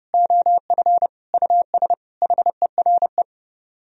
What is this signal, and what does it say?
Morse code, heard as a keyed tone, about 33 wpm, 700 Hz. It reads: OF US HERE